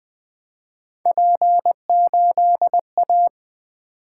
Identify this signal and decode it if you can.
Morse code, heard as a keyed tone, about 20 wpm, 700 Hz. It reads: P8A
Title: Three Letter-Number Combinations - 20wpm